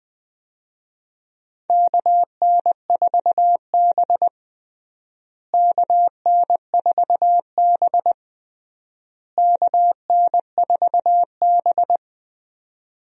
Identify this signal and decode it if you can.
Morse code, heard as a keyed tone, about 20 wpm, 700 Hz. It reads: KN4B KN4B KN4B
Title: USA Call Signs - 4 Characters - Repeated 3x - 20wpm